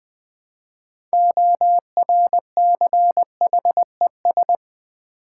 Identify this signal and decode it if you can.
Morse code, heard as a keyed tone, about 20 wpm, 700 Hz. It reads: ORCHES
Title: Build A Word - 3 Letters At A Time - 9-Letter Words - 20wpm